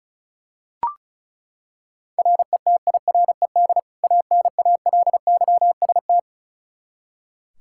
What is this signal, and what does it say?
Morse code, heard as a keyed tone, about 34 wpm, 700 Hz. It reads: RETIRED ANALYST
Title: Practice QSO - Exchange 3 - More Info - 35wpm